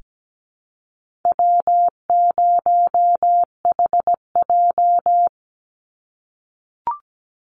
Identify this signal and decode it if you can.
Morse code, heard as a keyed tone, about 17 wpm, 700 Hz. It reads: W0HJ
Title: USA Call Signs 17wpm